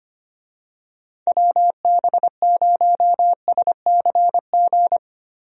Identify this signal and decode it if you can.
Morse code, heard as a keyed tone, about 25 wpm, 700 Hz. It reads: WB0SCG
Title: USA Call Signs - 6 Characters - Repeated 2x - 25wpm